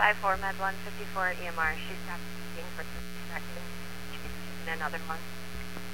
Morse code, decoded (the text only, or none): none